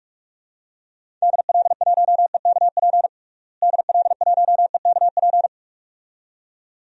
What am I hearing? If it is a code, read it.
Morse code, heard as a keyed tone, about 46 wpm, 700 Hz. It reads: DL1EKP DL1EKP
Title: DXCC-100 Call Signs - 6 Characters - Repeated 2x - 45wpm